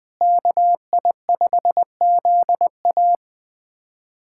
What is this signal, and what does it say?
Morse code, heard as a keyed tone, about 20 wpm, 700 Hz. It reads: KI5ZA
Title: USA Callsign DE Callsign - 5 Characters - 20wpm